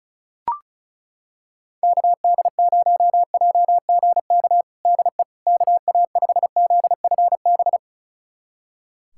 Morse code, heard as a keyed tone, about 34 wpm, 700 Hz. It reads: KD0JGK DE KA5ZFB
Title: USA Callsign DE Callsign - 6 Characters - 35wpm